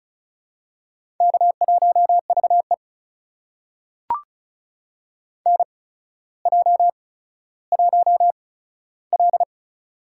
Morse code, heard as a keyed tone, about 36 wpm, spaced as for 15 wpm, 700 Hz. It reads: K1VE NJ1L
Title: USA Call Signs - 4 Characters - 8x Character Spacing - 35wpm